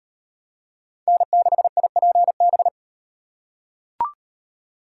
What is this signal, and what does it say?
Morse code, heard as a keyed tone, about 38 wpm, 700 Hz. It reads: N6IPB